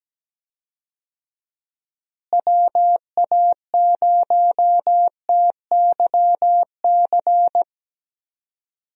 Morse code, heard as a keyed tone, about 17 wpm, 700 Hz. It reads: WA0TYC
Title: USA Call Signs - 6 Characters - Repeated 3x - 17wpm